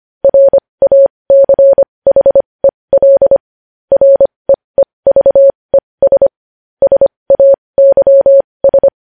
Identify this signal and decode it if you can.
Morse code, heard as a keyed tone, about 25 wpm, spaced as for 20 wpm, 550 Hz. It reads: RACHEL REEVES SAYS
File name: podcast / News headlines-1x-25x20